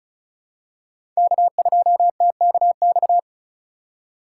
Morse code, heard as a keyed tone, about 34 wpm, 700 Hz. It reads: K2TKX
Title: USA Call Signs - 5 Characters - 2x Character Spacing - 35wpm